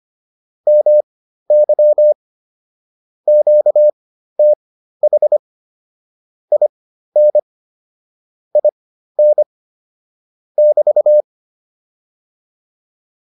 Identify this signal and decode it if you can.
Morse code, heard as a keyed tone, about 25 wpm, spaced as for 13 wpm, 600 Hz. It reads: MY QTH IN IN <BT>